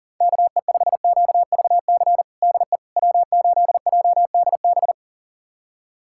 Morse code, heard as a keyed tone, about 40 wpm, 700 Hz. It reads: KE5QVC DE W8JDB
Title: USA Callsign DE Callsign 40wpm